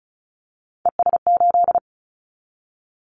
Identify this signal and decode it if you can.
Morse code, heard as a keyed tone, about 36 wpm, 700 Hz. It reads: ES8
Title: Callsign Prefixes for Top 50 DXCC - 35wpm